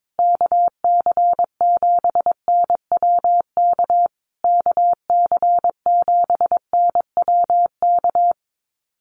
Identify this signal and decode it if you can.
Morse code, heard as a keyed tone, about 22 wpm, 700 Hz. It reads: KC7NWK KC7NWK
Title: USA Call Signs Repeated 3x 22wpm